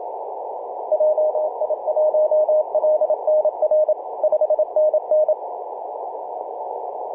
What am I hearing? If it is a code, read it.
Morse code, heard as a keyed tone, about 27 wpm, 620 Hz. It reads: JI1LNR 5NN
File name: 3d2se-18-080mhz-cw